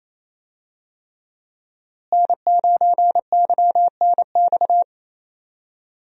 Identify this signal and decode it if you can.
Morse code, heard as a keyed tone, about 28 wpm, 700 Hz. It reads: N9YNX
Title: USA Call Signs - 5 Characters - Repeated 3x - 28wpm